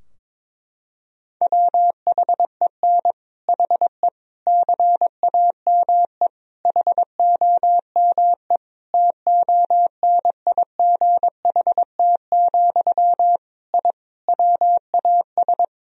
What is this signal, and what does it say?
Morse code, heard as a keyed tone, about 21 wpm, 700 Hz. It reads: WHEN HE CAME HOME TONIGHT, I WAS
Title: Sentences from Top 1000 Words - Review Part 3 - 22wpm